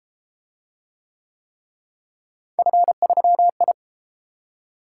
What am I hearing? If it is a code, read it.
Morse code, heard as a keyed tone, about 31 wpm, 700 Hz. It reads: F3I